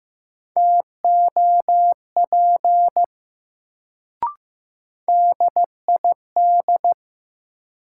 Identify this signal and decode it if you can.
Morse code, heard as a keyed tone, about 15 wpm, 700 Hz. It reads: TOP DID
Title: Sets of 2 Words - New Word Drill - 15wpm